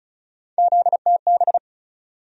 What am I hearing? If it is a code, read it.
Morse code, heard as a keyed tone, about 34 wpm, 700 Hz. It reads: ZTB
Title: USA Call Sign Postfixes - 3 Letters 35wpm